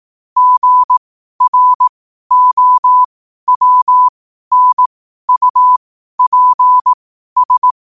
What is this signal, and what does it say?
Morse code, heard as a keyed tone, about 18 wpm, spaced as for 12 wpm, 1000 Hz. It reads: GROWNUPS
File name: qotdfiles / 12wpm